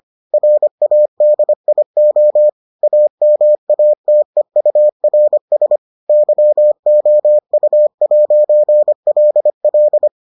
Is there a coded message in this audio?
Yes: Morse code from 0.3 to 10.1 s, heard as a keyed tone, about 25 wpm, 600 Hz. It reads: RADIO AMATEURS YOU'LL